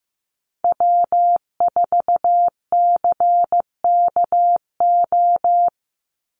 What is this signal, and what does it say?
Morse code, heard as a keyed tone, about 15 wpm, 700 Hz. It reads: W4CKO
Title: USA Call Signs 15wpm